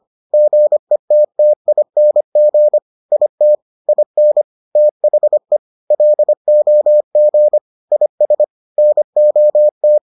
Morse code, heard as a keyed tone, about 25 wpm, 600 Hz. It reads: GETTING IT IN THE LOG IS NOT